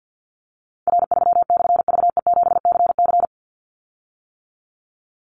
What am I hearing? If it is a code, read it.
Morse code, heard as a keyed tone, about 50 wpm, 700 Hz. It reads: R3/VE7CC
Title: Call Signs - 50wpm